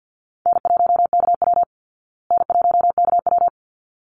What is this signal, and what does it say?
Morse code, heard as a keyed tone, about 50 wpm, 700 Hz. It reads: N1KW N1KW
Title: Easy Call Signs Repeated 2x - 50wpm